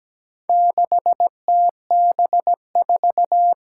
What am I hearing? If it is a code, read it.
Morse code, heard as a keyed tone, about 17 wpm, 700 Hz. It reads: 6TB4
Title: Four Challenging Letter-Number Combinations 17wpm